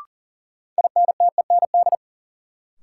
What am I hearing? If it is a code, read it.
Morse code, heard as a keyed tone, about 40 wpm, 700 Hz. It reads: INTEND
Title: Sentences from Top 1600 Words - New Word Drill Alt - 40wpm